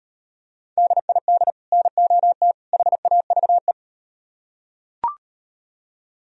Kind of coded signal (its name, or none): Morse code